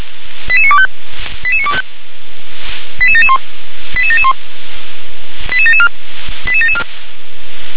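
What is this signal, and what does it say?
Five-tone selective call (ZVEI), heard as quick runs of five tones, tones of about 70 ms. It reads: 88925 88925 88771 88771 88774 88774
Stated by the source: ZVEI five-tone selective call received with SDR.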